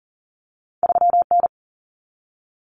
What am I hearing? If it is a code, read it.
Morse code, heard as a keyed tone, about 40 wpm, 700 Hz. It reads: 3N
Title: Common Cut Numbers - 2-Digit 40wpm